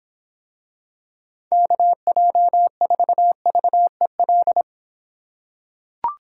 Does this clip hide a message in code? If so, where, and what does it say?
Yes: Morse code from 1.5 to 4.6 s, heard as a keyed tone, about 26 wpm, 700 Hz. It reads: KJ4VEL